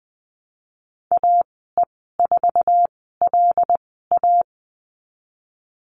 Morse code, heard as a keyed tone, about 20 wpm, spaced as for 15 wpm, 700 Hz. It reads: AE4LA
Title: 088 Lesson 15g - Number 5 - Call Signs 20wpm